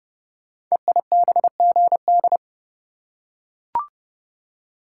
Morse code, heard as a keyed tone, about 30 wpm, 700 Hz. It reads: EIBGD
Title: Code Groups - 5 Letters - 30wpm